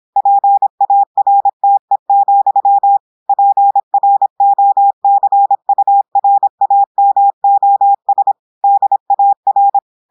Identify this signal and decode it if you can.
Morse code, heard as a keyed tone, about 26 wpm, 800 Hz. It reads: PARTE, PROCURAMOS DAR
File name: audios / portugais / portugais texte en clair 1 vitesse 26 mots minutes